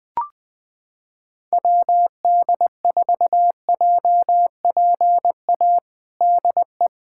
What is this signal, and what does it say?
Morse code, heard as a keyed tone, about 20 wpm, 700 Hz. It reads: WD4JPA DE
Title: USA Callsign DE Callsign 20wpm